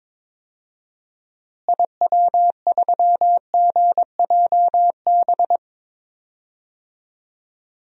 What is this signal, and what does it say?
Morse code, heard as a keyed tone, about 22 wpm, 700 Hz. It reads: IW3GJB
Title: DXCC-100 Call Signs - 6 Characters - Repeated 2x - 22wpm